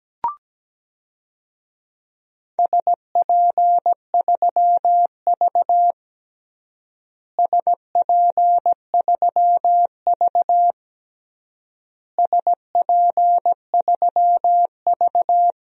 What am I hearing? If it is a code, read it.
Morse code, heard as a keyed tone, about 17 wpm, 700 Hz. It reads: SP3V SP3V SP3V
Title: DXCC-100 Call Signs - 4 Characters - Repeated 3x - 17wpm